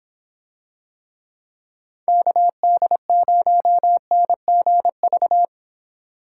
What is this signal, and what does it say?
Morse code, heard as a keyed tone, about 26 wpm, 700 Hz. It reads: KD0NGV